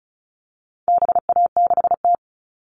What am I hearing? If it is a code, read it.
Morse code, heard as a keyed tone, about 34 wpm, 700 Hz. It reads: BA6T
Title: All Cut Numbers - 4-Digit 35wpm